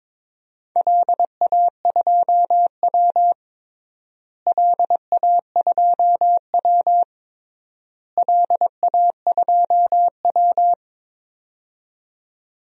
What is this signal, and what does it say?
Morse code, heard as a keyed tone, about 22 wpm, 700 Hz. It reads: LA2W LA2W LA2W
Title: DXCC-100 Call Signs - 4 Characters - Repeated 3x - 22wpm